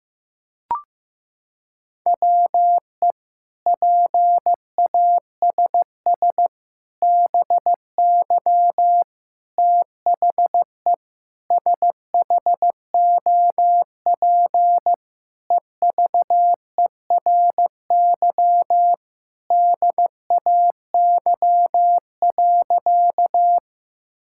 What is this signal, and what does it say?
Morse code, heard as a keyed tone, about 15 wpm, 700 Hz. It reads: WE PASS BY THE SHOP EVERY DAY.
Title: Sentences from Top 1600 Words 15wpm